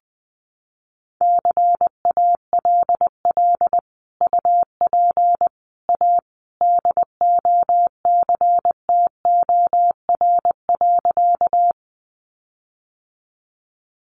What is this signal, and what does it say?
Morse code, heard as a keyed tone, about 20 wpm, 700 Hz. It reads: CALL UP A DOCTOR.
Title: Sentences from Top 1500 Words 20wpm